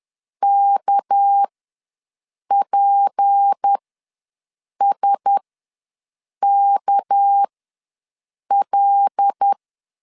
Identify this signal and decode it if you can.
Morse code, heard as a keyed tone, about 10 wpm, spaced as for 6 wpm, 800 Hz. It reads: KPSKL
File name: LEZIONI / 6 PKX0 / 6 PK ws 10wpm 3745